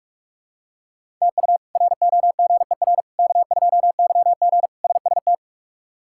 Morse code, heard as a keyed tone, about 43 wpm, 700 Hz. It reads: TU ROGER K2YG SST
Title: K1USN SST QSO Practice - 45wpm